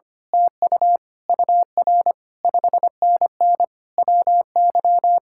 Morse code, heard as a keyed tone, about 24 wpm, 700 Hz. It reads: TU UR 5NN WY
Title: Speed-Racing Common QSO Phrases Encore - 25wpm